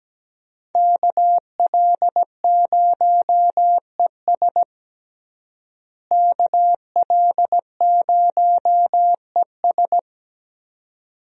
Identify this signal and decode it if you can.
Morse code, heard as a keyed tone, about 17 wpm, 700 Hz. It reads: KL0ES KL0ES
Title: USA Call Signs - 5 Characters - Repeated 2x - 17wpm